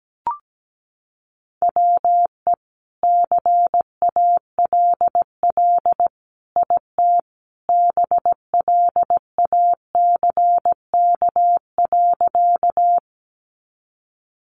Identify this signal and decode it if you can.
Morse code, heard as a keyed tone, about 17 wpm, 700 Hz. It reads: WE CALL IT BLACK.
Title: Sentences from Top 400 Words 17wpm